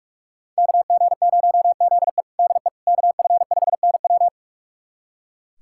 Morse code, heard as a keyed tone, about 46 wpm, 700 Hz. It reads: KG0ZE DE KF5NW
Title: USA Callsign DE Callsign - 5 Characters - 45wpm